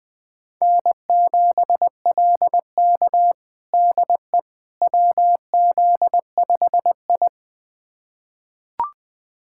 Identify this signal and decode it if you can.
Morse code, heard as a keyed tone, about 20 wpm, 700 Hz. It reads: N7LK DE WZ5I
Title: USA Callsign DE Callsign - 4 Characters - 20wpm